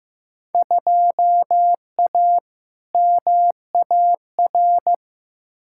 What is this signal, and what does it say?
Morse code, heard as a keyed tone, about 15 wpm, 700 Hz. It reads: 2A MAR
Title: Field Day Exchanges - 15wpm